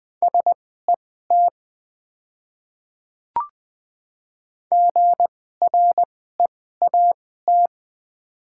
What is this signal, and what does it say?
Morse code, heard as a keyed tone, about 20 wpm, spaced as for 14 wpm, 700 Hz. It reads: SET GREAT
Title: 2x Character Spacing - Top 200 Words - 20wpm